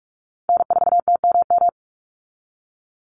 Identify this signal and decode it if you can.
Morse code, heard as a keyed tone, about 44 wpm, 700 Hz. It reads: N4TMM
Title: Call Signs Repeated 3x - 45wpm